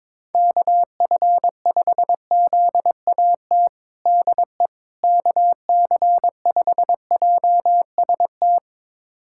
Morse code, heard as a keyed tone, about 22 wpm, 700 Hz. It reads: KF5ZAT DE KC5JST
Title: USA Callsign DE Callsign - 6 Characters - 22wpm